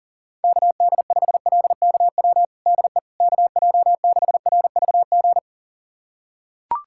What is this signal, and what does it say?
Morse code, heard as a keyed tone, about 40 wpm, 700 Hz. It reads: KD5LKW DE KJ6RVG
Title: USA Callsign DE Callsign - 6 Characters - 40wpm